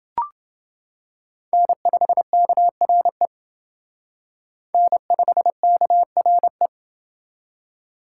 Morse code, heard as a keyed tone, about 28 wpm, 700 Hz. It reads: N5KRE N5KRE